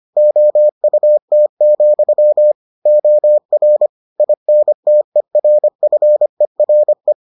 Morse code, heard as a keyed tone, about 25 wpm, 600 Hz. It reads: OUT, OR INTERFERE